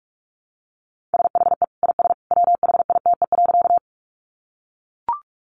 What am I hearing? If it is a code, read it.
Morse code, heard as a keyed tone, about 44 wpm, 700 Hz. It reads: SHE IS WHITE.
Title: Easy Phrases 45wpm